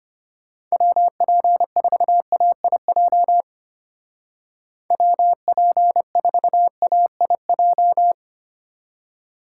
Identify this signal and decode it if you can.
Morse code, heard as a keyed tone, about 32 wpm, 700 Hz. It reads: WP4AIJ WP4AIJ